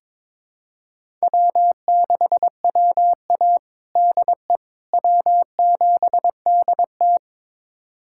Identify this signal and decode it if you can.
Morse code, heard as a keyed tone, about 22 wpm, 700 Hz. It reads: W6WA DE W7DT
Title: USA Callsign DE Callsign - 4 Characters - 22wpm